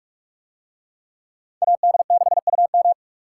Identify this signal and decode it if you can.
Morse code, heard as a keyed tone, about 42 wpm, 700 Hz. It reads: AD6UM